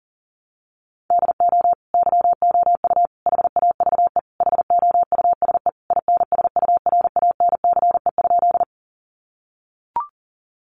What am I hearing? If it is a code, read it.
Morse code, heard as a keyed tone, about 40 wpm, 700 Hz. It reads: DO YOU HAVE HOUSE INSURANCE?
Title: Sentences from Top 1100 Words 40wpm